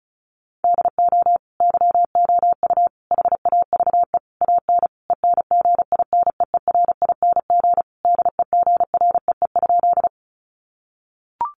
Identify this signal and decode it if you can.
Morse code, heard as a keyed tone, about 34 wpm, 700 Hz. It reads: DO YOU HAVE AN ENGINEERING DEGREE?
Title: Sentences from Top 1500 Words 35wpm